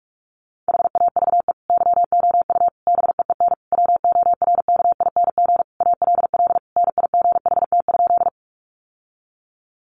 Morse code, heard as a keyed tone, about 44 wpm, 700 Hz. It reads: HAVE YOU BEEN WORKING ALL NIGHT?
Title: Sentences from Top 700 Words 45wpm